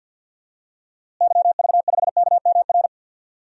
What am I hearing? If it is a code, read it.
Morse code, heard as a keyed tone, about 46 wpm, 700 Hz. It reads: YV5KMR
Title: DXCC-100 Call Signs - 6 Characters - Repeated 3x - 50wpm